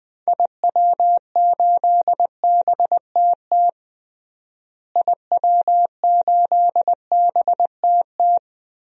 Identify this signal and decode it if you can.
Morse code, heard as a keyed tone, about 20 wpm, 700 Hz. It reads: IW8BTT IW8BTT
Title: DXCC-100 Call Signs - 6 Characters - Repeated 3x - 20wpm